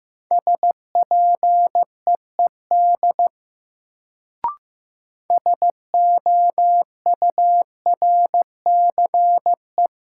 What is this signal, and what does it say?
Morse code, heard as a keyed tone, about 15 wpm, 700 Hz. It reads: SPEED SOURCE
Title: Sentences from Top 800 Words - New Word Drill - 15wpm